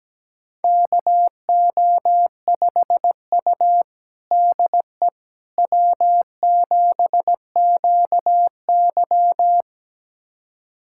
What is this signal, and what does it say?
Morse code, heard as a keyed tone, about 17 wpm, 700 Hz. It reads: KO5U DE W7QY
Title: USA Callsign DE Callsign - 4 Characters - 17wpm